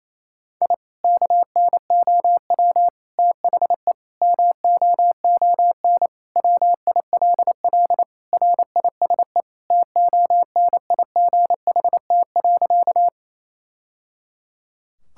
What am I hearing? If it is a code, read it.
Morse code, heard as a keyed tone, about 28 wpm, 700 Hz. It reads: I KNOW THE MOON WILL RISE TONIGHT.
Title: Sentences from Top 1000 Words - Review Part 6 - 28wpm